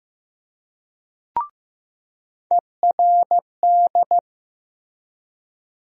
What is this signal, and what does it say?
Morse code, heard as a keyed tone, about 15 wpm, 700 Hz. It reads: ERD
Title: USA Call Sign Postfixes - 3 Letters 15wpm